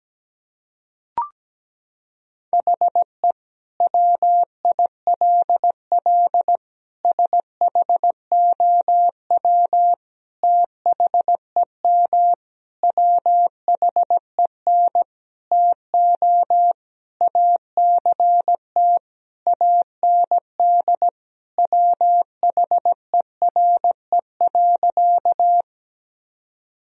Morse code, heard as a keyed tone, about 17 wpm, 700 Hz. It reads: HE WILL SHOW THEM WHEN TO ACT AND WHERE.
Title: Sentences from Top 1000 Words - Review Part 1 - 17wpm